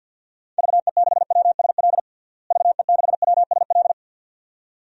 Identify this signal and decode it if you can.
Morse code, heard as a keyed tone, about 47 wpm, 700 Hz. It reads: VE6WSL VE6WSL
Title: DXCC-100 Call Signs - 6 Characters - Repeated 2x - 50wpm